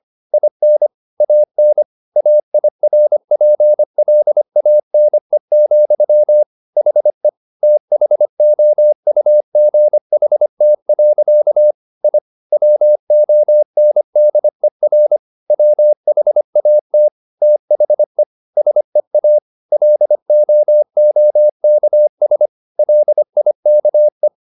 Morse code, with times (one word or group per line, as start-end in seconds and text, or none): 0.3-0.9 IN
1.2-1.8 AN
2.2-6.4 AIRPLANE,
6.8-7.3 HE
7.6-11.7 THOUGHT.
12.0-12.2 I
12.5-15.2 WONDER
15.5-17.1 WHAT
17.4-18.2 THE
18.6-19.4 SEA
19.7-22.5 LOOKS
22.8-24.3 LIKE